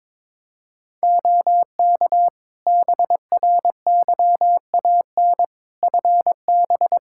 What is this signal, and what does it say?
Morse code, heard as a keyed tone, about 22 wpm, 700 Hz. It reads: OK BRYAN FB
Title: Practice QSO - Exchange 2 - Info - 22wpm